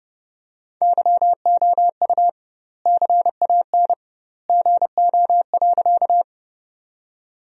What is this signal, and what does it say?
Morse code, heard as a keyed tone, about 30 wpm, 700 Hz. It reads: YOU CAN GO.
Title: 2x Word-Spacing - Sentences from Top 100 Words - 30wpm